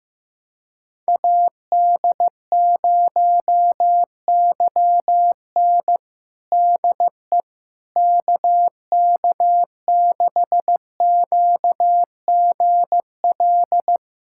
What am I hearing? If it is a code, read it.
Morse code, heard as a keyed tone, about 15 wpm, 700 Hz. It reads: AD0YN DE KK6QGL
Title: USA Callsign DE Callsign 15wpm